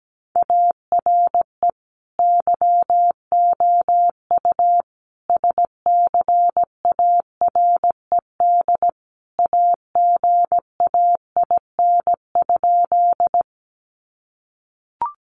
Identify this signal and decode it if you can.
Morse code, heard as a keyed tone, about 17 wpm, 700 Hz. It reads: ARE YOU SCARED AGAIN?
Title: Sentences from Top 700 Words 17wpm